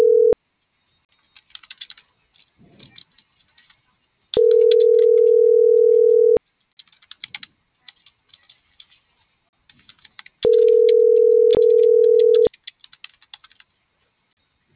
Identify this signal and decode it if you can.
Ringback tone (North America), 440 + 480 Hz.